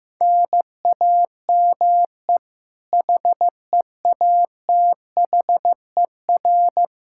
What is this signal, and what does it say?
Morse code, heard as a keyed tone, about 15 wpm, 700 Hz. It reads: NAME HEATHER
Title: K1USN SST QSO Practice 15wpm